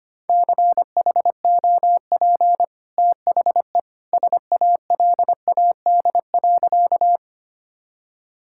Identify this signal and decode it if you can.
Morse code, heard as a keyed tone, about 25 wpm, 700 Hz. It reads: CHOP THE SALAD.
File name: Easy Sentences with Hints - 25wpm